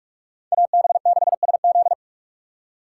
Morse code, heard as a keyed tone, about 39 wpm, 700 Hz. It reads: AB6SZ